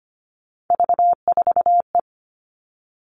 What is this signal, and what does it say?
Morse code, heard as a keyed tone, about 25 wpm, 700 Hz. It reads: V4E
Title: Three Challenging Letter-Number Combinations 25wpm